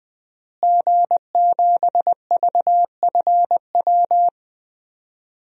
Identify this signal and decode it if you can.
Morse code, heard as a keyed tone, about 20 wpm, 700 Hz. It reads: G7VFW
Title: DXCC-100 Call Signs - 5 Characters - Repeated 3x - 20wpm